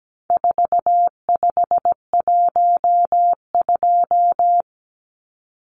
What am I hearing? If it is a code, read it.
Morse code, heard as a keyed tone, about 17 wpm, 700 Hz. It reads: 4512
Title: Numbers - 4-Digit 17wpm